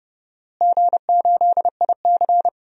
Morse code, heard as a keyed tone, about 30 wpm, 700 Hz. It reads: G8IC
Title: DXCC-100 Call Signs - 4 Characters - Repeated 2x - 30wpm